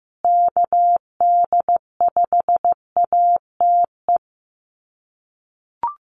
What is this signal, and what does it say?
Morse code, heard as a keyed tone, about 15 wpm, 700 Hz. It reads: KD5ATE
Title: USA Call Signs Repeated 2x 15wpm